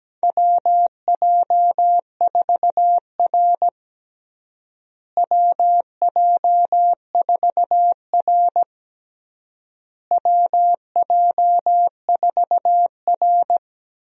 Morse code, heard as a keyed tone, about 17 wpm, 700 Hz. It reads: WJ4R WJ4R WJ4R
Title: USA Call Signs - 4 Characters - Repeated 3x - 17wpm